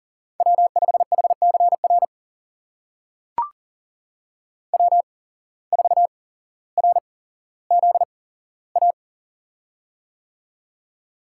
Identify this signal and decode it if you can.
Morse code, heard as a keyed tone, about 39 wpm, spaced as for 17 wpm, 700 Hz. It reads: W5HCR W4RZA
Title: USA Call Signs - 5 Characters - 8x Character Spacing - 40wpm